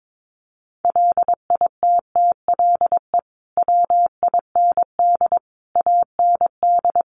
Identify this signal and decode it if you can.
Morse code, heard as a keyed tone, about 22 wpm, 700 Hz. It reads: LITTLE WIND AND
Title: Sentences from Top 1300 Words 22wpm